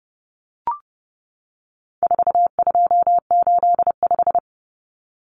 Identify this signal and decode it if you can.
Morse code, heard as a keyed tone, about 30 wpm, 700 Hz. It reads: 4285
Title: Common Cut Numbers - 4-Digit 30wpm